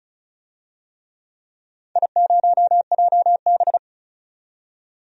Morse code, heard as a keyed tone, about 36 wpm, 700 Hz. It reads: I0JB
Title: DXCC-100 Call Signs - 4 Characters - 35wpm